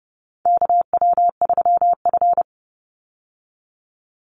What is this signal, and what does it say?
Morse code, heard as a keyed tone, about 30 wpm, 700 Hz. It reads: KW3F
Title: Call Signs Repeated 2x - 30wpm